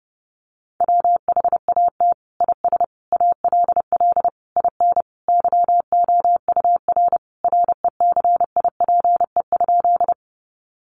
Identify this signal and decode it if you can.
Morse code, heard as a keyed tone, about 30 wpm, 700 Hz. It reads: WHAT IS ALL IN YOUR RECIPE?
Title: Sentences from Top 1200 Words 30wpm